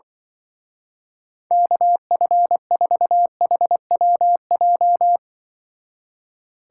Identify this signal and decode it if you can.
Morse code, heard as a keyed tone, about 24 wpm, 700 Hz. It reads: KF4HWJ
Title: Speed Racing - USA 2x3 Call Signs 20wpm